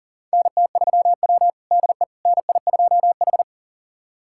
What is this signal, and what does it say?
Morse code, heard as a keyed tone, about 40 wpm, 700 Hz. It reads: NT3W DE NI2H
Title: USA Callsign DE Callsign - 4 Characters - 40wpm